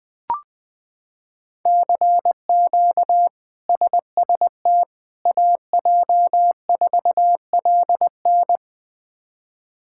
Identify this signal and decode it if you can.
Morse code, heard as a keyed tone, about 21 wpm, 700 Hz. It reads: CQ SST AJ4LN
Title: K1USN SST QSO Practice - 20wpm